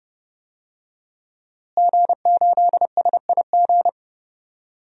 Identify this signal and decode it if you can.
Morse code, heard as a keyed tone, about 32 wpm, 700 Hz. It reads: G8SIG
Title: DXCC-100 Call Signs - 5 Characters - Repeated 3x - 30wpm